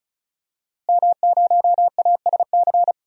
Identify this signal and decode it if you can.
Morse code, heard as a keyed tone, about 36 wpm, 700 Hz. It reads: M0ASC
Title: DXCC-100 Call Signs - 5 Characters - Repeated 2x - 35wpm